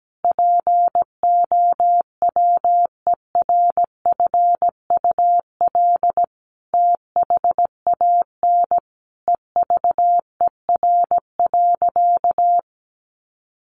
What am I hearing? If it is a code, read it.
Morse code, heard as a keyed tone, about 17 wpm, 700 Hz. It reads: POWERFUL THAN EVER.
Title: Sentences from Top 1300 Words 17wpm